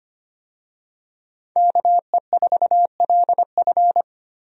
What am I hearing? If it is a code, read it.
Morse code, heard as a keyed tone, about 25 wpm, 700 Hz. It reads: KE4LF
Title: USA Call Signs - 5 Characters - Repeated 2x - 25wpm